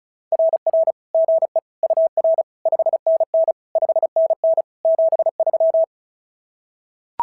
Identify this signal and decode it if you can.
Morse code, heard as a keyed tone, about 36 wpm, 660 Hz. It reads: RR GE UR 5NN 5NN 73
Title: SOTA QSO Practice 35wpm